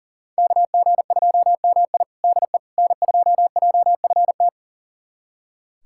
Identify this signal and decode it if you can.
Morse code, heard as a keyed tone, about 40 wpm, 700 Hz. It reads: KG2MI DE N2JFT
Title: USA Callsign DE Callsign 40wpm